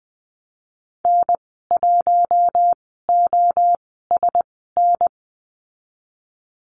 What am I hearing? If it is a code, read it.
Morse code, heard as a keyed tone, about 20 wpm, spaced as for 14 wpm, 700 Hz. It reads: N1OSN
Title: 048 Lesson 9g - Number 4 - Call Signs 20wpm